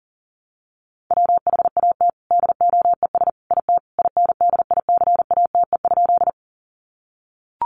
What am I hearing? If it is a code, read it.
Morse code, heard as a keyed tone, about 40 wpm, 700 Hz. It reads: WHAT DOES IT INDICATE?
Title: Sentences from Top 1300 Words 40wpm